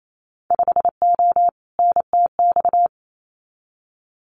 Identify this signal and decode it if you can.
Morse code, heard as a keyed tone, about 28 wpm, 700 Hz. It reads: 5O NTX
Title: Winter Field Day Exchanges Encore 28wpm